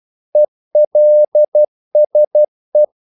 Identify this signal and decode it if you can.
Morse code, heard as a keyed tone, about 12 wpm, 600 Hz. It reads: ELSE
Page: English quotations in CW (telegraphy - morse)